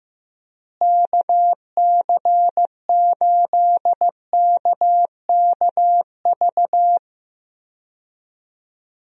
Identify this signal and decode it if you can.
Morse code, heard as a keyed tone, about 15 wpm, 700 Hz. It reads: KC8KKV
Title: USA Call Signs - 6 Characters - Repeated 2x - 15wpm